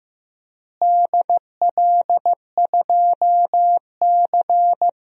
Morse code, heard as a keyed tone, about 15 wpm, 700 Hz. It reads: DL2C
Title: DXCC-100 Call Signs - 4 Characters - Repeated 2x - 15wpm